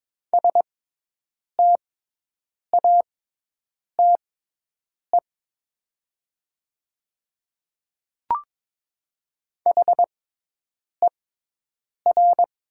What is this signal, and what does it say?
Morse code, heard as a keyed tone, about 22 wpm, spaced as for 8 wpm, 700 Hz. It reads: STATE HER
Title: 6x Character Spacing - Top 500 Words - 22wpm